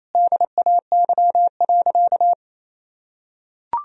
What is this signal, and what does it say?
Morse code, heard as a keyed tone, about 28 wpm, 700 Hz. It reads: DAY.
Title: Sentences from Top 1000 Words - Review Part 4 - 28wpm